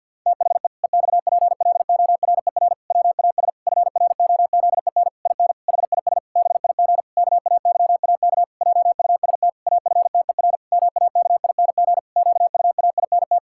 Morse code, heard as a keyed tone, about 50 wpm, 700 Hz. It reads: THE EXPLORER WAS FROZEN IN HIS BIG KAYAK JUST AFTER MAKING QUAINT
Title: Pangram Challenge v2 - 50wpm